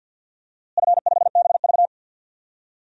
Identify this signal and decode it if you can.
Morse code, heard as a keyed tone, about 46 wpm, 700 Hz. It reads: F5BV
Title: DXCC-100 Call Signs - 4 Characters - Repeated 3x - 50wpm